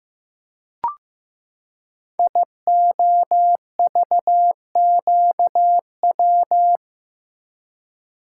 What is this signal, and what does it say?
Morse code, heard as a keyed tone, about 15 wpm, 700 Hz. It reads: IOVQW
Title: Code Groups - 5 Letter:Numbers - 15wpm